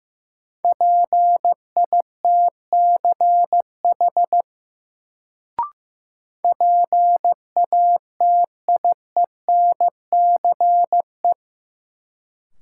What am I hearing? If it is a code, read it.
Morse code, heard as a keyed tone, about 15 wpm, 700 Hz. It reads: PITCH PATIENCE
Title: Sentences from Top 1300 Words - New Word Drill Alt - 15wpm